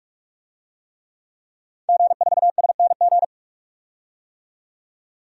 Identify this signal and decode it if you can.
Morse code, heard as a keyed tone, about 44 wpm, 690 Hz. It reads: G4SNG
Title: DXCC-100 Call Signs - 5 Characters - 45wpm